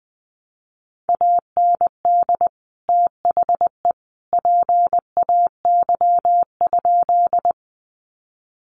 Morse code, heard as a keyed tone, about 20 wpm, 700 Hz. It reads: AND THE PAY?
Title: Sentences from Top 700 Words 20wpm